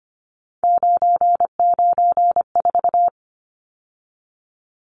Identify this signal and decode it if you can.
Morse code, heard as a keyed tone, about 25 wpm, 700 Hz. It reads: 994
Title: Numbers - 3-Digit 25wpm